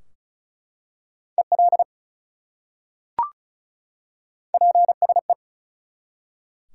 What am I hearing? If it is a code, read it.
Morse code, heard as a keyed tone, about 36 wpm, 700 Hz. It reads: EL PSE
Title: Top 180 QSO Elements 35wpm